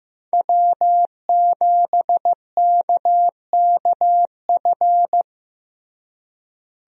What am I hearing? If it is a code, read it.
Morse code, heard as a keyed tone, about 15 wpm, 700 Hz. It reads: W7KKF
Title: USA Call Signs - 5 Characters - Repeated 2x - 15wpm